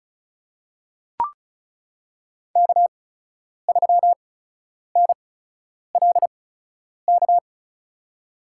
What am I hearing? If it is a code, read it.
Morse code, heard as a keyed tone, about 36 wpm, spaced as for 10 wpm, 700 Hz. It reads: K3NLK
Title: USA Call Signs - 5 Characters - 8x Character Spacing - 35wpm